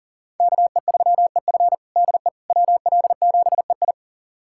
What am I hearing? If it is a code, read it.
Morse code, heard as a keyed tone, about 37 wpm, 700 Hz. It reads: KE3EF DE WL7EI
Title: USA Callsign DE Callsign - 5 Characters - 40wpm